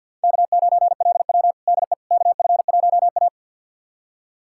Morse code, heard as a keyed tone, about 50 wpm, 700 Hz. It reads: K9LW DE KF1A
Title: USA Callsign DE Callsign - 4 Characters - 50wpm